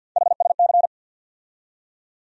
Morse code, heard as a keyed tone, about 53 wpm, 700 Hz. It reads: HSC
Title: USA Call Sign Postfixes - 3 Letters 50wpm